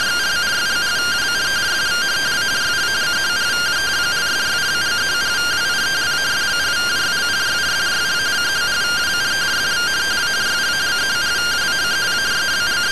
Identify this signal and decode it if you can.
Two-tone teleprinter signal, about 170 Hz shift.